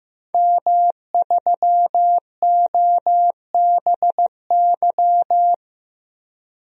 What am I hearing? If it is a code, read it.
Morse code, heard as a keyed tone, about 15 wpm, 700 Hz. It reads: M3OBY
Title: DXCC-100 Call Signs - 5 Characters - 15wpm